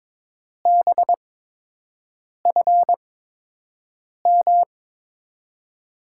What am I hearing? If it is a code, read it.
Morse code, heard as a keyed tone, about 22 wpm, 700 Hz. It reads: B F M
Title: USA Call Signs - 6 Characters - 8x Character Spacing - 22wpm